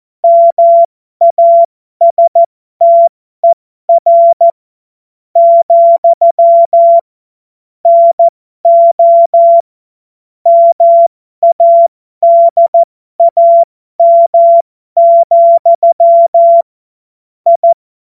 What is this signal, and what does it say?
Morse code, heard as a keyed tone, about 14 wpm, spaced as for 11 wpm, 680 Hz. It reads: MASTER, NO MADAM, I
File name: Morsekurs / Witze